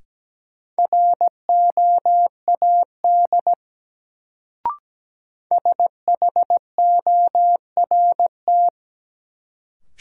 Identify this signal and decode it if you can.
Morse code, heard as a keyed tone, about 17 wpm, 700 Hz. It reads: ROAD SHORT
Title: Sentences from Top 400 Words - New Word Drill - 17wpm